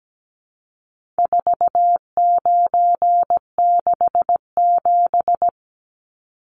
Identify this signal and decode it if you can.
Morse code, heard as a keyed tone, about 17 wpm, 700 Hz. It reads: 4967
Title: Numbers - 4-Digit 17wpm